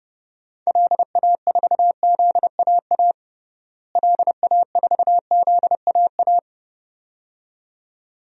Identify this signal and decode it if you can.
Morse code, heard as a keyed tone, about 30 wpm, 700 Hz. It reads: LA4ZAA LA4ZAA
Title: DXCC-100 Call Signs - 6 Characters - Repeated 2x - 30wpm